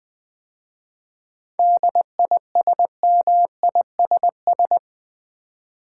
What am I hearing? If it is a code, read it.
Morse code, heard as a keyed tone, about 20 wpm, 700 Hz. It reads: DISMISS
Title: 7 Letter Words - First 3 Letters Given - 20wpm